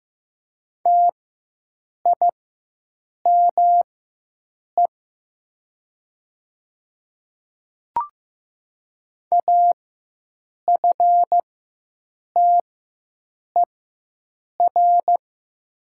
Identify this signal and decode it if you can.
Morse code, heard as a keyed tone, about 15 wpm, spaced as for 7 wpm, 700 Hz. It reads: TIME AFTER
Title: 4x Character Spacing - Top 300 Words - 15wpm